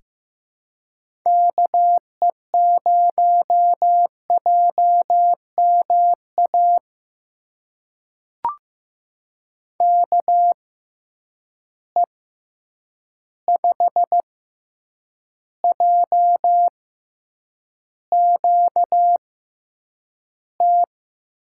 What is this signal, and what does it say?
Morse code, heard as a keyed tone, about 15 wpm, spaced as for 8 wpm, 700 Hz. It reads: KE0JMA KE5JQT
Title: USA Call Signs - 6 Characters - 6x Character Spacing - 15wpm